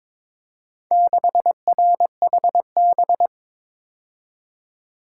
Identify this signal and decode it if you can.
Morse code, heard as a keyed tone, about 22 wpm, 690 Hz. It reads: 6RHB
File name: practice / 22wpm / Four Letter-Number Combinations - 22wpm